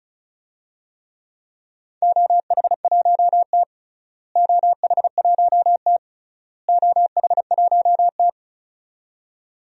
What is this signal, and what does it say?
Morse code, heard as a keyed tone, about 36 wpm, 700 Hz. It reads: OH1T OH1T OH1T
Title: DXCC-100 Call Signs - 4 Characters - Repeated 3x - 35wpm